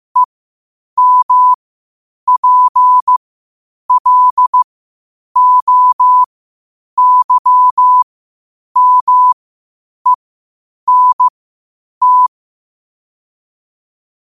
Morse code, heard as a keyed tone, about 15 wpm, spaced as for 8 wpm, 1000 Hz. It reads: EMPLOYMENT
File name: qotdfiles / 8wpm